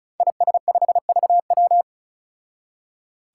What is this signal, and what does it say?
Morse code, heard as a keyed tone, about 36 wpm, 700 Hz. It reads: IS5VW
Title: Five Challenging Letter-Number Combinations 35wpm